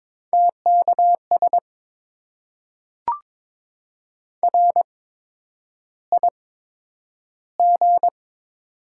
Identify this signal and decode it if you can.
Morse code, heard as a keyed tone, about 21 wpm, spaced as for 9 wpm, 700 Hz. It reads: TKS RIG
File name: Top 90 QSO Elements - 8x Character Spacing - 22wpm